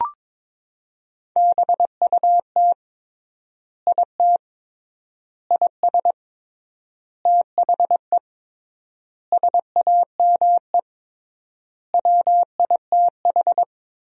Morse code, heard as a keyed tone, about 22 wpm, 700 Hz. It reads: BUT IT IS THE SAME WITH
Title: 3x Word-Spacing - Sentences from Top 300 Words - 22wpm